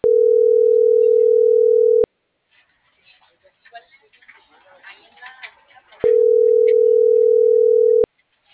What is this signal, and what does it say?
Ringback tone (North America), 440 + 480 Hz.